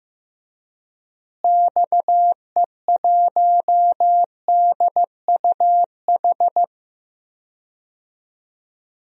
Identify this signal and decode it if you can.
Morse code, heard as a keyed tone, about 15 wpm, 700 Hz. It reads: XE1DUH
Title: DXCC-100 Call Signs - 6 Characters - Repeated 2x - 15wpm